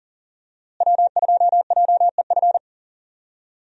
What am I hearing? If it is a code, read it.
Morse code, heard as a keyed tone, about 40 wpm, 700 Hz. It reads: W2JEF
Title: USA Call Signs - 5 Characters - Repeated 2x - 40wpm